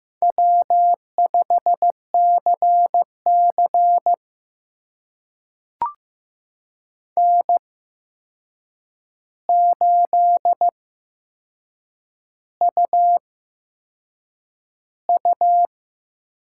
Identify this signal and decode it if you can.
Morse code, heard as a keyed tone, about 15 wpm, spaced as for 6 wpm, 700 Hz. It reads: W5CC N8UU
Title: USA Call Signs - 8x Character Spacing - 15wpm